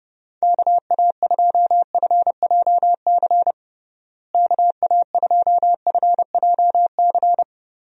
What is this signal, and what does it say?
Morse code, heard as a keyed tone, about 32 wpm, 700 Hz. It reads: KA2FJC KA2FJC
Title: USA Call Signs - 6 Characters - Repeated 2x - 30wpm